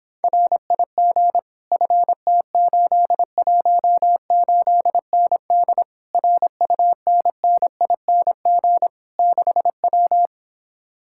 Morse code, heard as a keyed tone, about 26 wpm, 700 Hz. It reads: RIG FT818ND RUNNING 6W